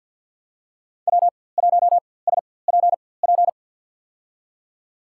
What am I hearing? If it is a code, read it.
Morse code, heard as a keyed tone, about 53 wpm, spaced as for 24 wpm, 700 Hz. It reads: W1SPP
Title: USA Call Signs - 5 Characters - 4x Character Spacing - 50wpm